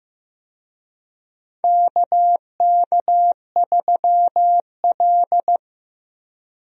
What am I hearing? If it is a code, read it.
Morse code, heard as a keyed tone, about 15 wpm, 700 Hz. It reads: KK3L
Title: USA Call Signs - 4 Characters - Repeated 2x - 15wpm